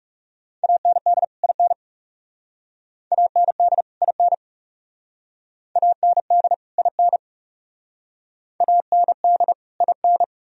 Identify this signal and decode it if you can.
Morse code, heard as a keyed tone, about 36 wpm, 700 Hz. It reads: AND IN AND IN AND IN AND IN
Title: Speed-Racing - Sets of 2 Words 30wpm